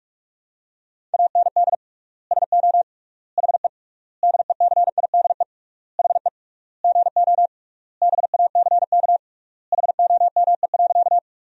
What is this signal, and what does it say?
Morse code, heard as a keyed tone, about 41 wpm, 700 Hz. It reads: AND SO HE DECIDE HE GO BACK HOME.
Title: 3x Word-Spacing - Sentences from Top 500 Words - 45wpm